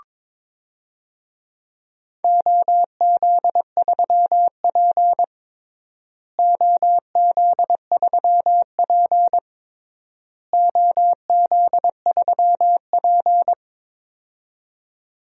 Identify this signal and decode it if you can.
Morse code, heard as a keyed tone, about 22 wpm, 700 Hz. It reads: OZ3P OZ3P OZ3P
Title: DXCC-100 Call Signs - 4 Characters - Repeated 3x - 22wpm